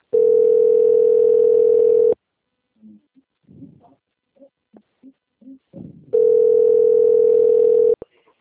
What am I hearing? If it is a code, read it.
Ringback tone (North America), 440 + 480 Hz.
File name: out- - - - - .1383